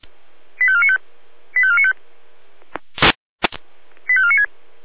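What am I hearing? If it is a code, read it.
Five-tone selective call (ZVEI), heard as quick runs of five tones, tones of about 70 ms. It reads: 85495 85495 85496
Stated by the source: Five-tone ZVEI selective call received on SDR.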